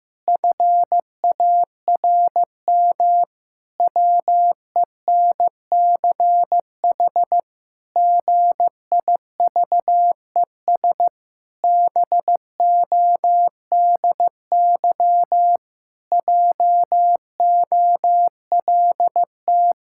Morse code, heard as a keyed tone, about 16 wpm, 700 Hz. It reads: FARM WENCH GIVES BODY JOLT
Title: Pangram Challenge v2 - 15wpm